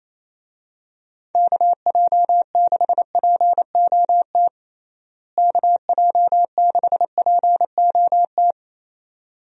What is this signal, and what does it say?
Morse code, heard as a keyed tone, about 28 wpm, 700 Hz. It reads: KJ6POT KJ6POT
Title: USA Call Signs - 6 Characters - Repeated 2x - 28wpm